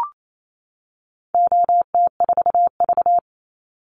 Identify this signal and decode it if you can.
Morse code, heard as a keyed tone, about 28 wpm, 700 Hz. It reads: OT4V
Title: Call Signs - 28wpm